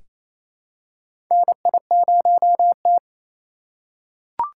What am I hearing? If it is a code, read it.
Morse code, heard as a keyed tone, about 27 wpm, 700 Hz. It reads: NI0T
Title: USA Call Signs - 6x Character Spacing - 28wpm